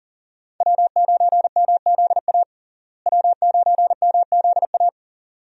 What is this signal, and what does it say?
Morse code, heard as a keyed tone, about 40 wpm, 700 Hz. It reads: W9MZA W9MZA
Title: USA Call Signs - 5 Characters - Repeated 2x - 40wpm